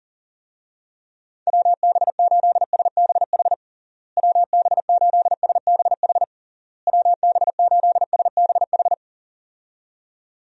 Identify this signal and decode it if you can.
Morse code, heard as a keyed tone, about 40 wpm, 700 Hz. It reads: WB8SBH WB8SBH WB8SBH
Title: USA Call Signs - 6 Characters - Repeated 3x - 40wpm